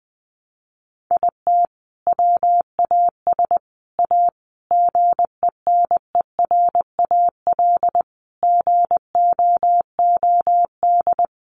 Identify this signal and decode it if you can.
Morse code, heard as a keyed tone, about 20 wpm, 700 Hz. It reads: IT WAS A GENERAL GOOD
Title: Sentences from Top 600 Words 20wpm